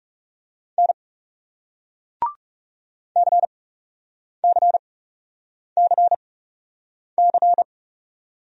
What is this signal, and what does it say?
Morse code, heard as a keyed tone, about 40 wpm, spaced as for 8 wpm, 700 Hz. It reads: N CCCC